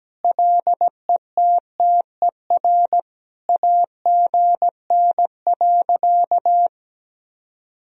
Morse code, heard as a keyed tone, about 17 wpm, 700 Hz. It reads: LETTER AGN.
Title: Sentences from Top 300 Words and Abbreviations 17wpm